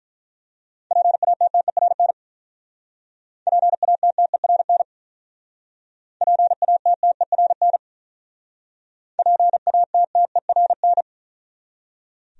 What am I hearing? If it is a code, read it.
Morse code, heard as a keyed tone, about 48 wpm, 700 Hz. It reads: PATTERN PATTERN PATTERN PATTERN